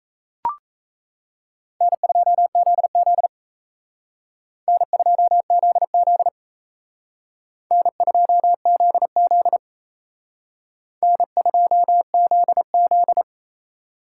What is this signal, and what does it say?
Morse code, heard as a keyed tone, about 34 wpm, 700 Hz. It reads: N2ZZ N2ZZ N2ZZ N2ZZ
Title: Speed Racing - USA 1x2 Call Signs 28wpm